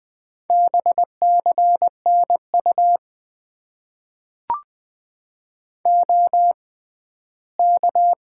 Morse code, heard as a keyed tone, about 20 wpm, spaced as for 14 wpm, 700 Hz. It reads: BCNU OK
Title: Top 90 QSO Elements - 6x Character Spacing - 20wpm